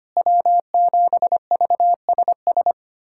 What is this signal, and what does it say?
Morse code, heard as a keyed tone, about 25 wpm, 700 Hz. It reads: W7VSS
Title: USA Callsign DE Callsign - 5 Characters - 25wpm